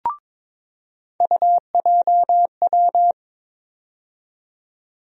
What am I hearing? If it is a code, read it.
Morse code, heard as a keyed tone, about 22 wpm, 700 Hz. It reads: UJW
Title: USA Call Sign Postfixes - 3 Letters 22wpm